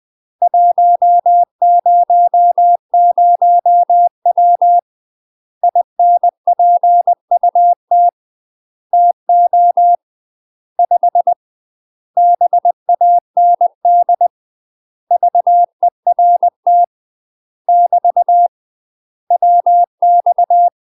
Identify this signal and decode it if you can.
Morse code, heard as a keyed tone, about 20 wpm, 700 Hz. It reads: 100W INPUT TO 5 BAND VERT <BT> WX
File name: UK MASTER QSOs TRAINING / 20WPM / UK MASTER QSOs 0042